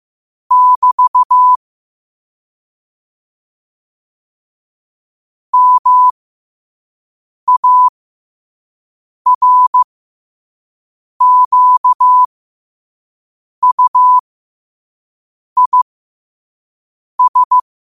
Morse code, heard as a keyed tone, about 15 wpm, spaced as for 5 wpm, 1000 Hz. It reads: <BT> MARQUIS
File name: qotdfiles / 5wpm